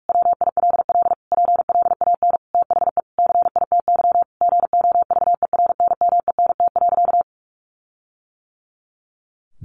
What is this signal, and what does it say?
Morse code, heard as a keyed tone, about 44 wpm, 700 Hz. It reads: WILL PLAN THE CITY GOVERNMENT.
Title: Sentences from Top 1400 Words 45wpm